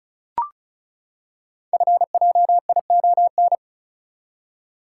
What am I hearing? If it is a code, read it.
Morse code, heard as a keyed tone, about 36 wpm, 690 Hz. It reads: FJION
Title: Code Groups - 5 Letters - 35wpm